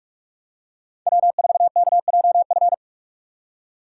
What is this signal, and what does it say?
Morse code, heard as a keyed tone, about 46 wpm, 700 Hz. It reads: W4KJF